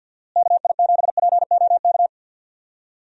Morse code, heard as a keyed tone, about 50 wpm, 700 Hz. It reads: KI7POK
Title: USA Call Signs - 6 Characters - Repeated 3x - 50wpm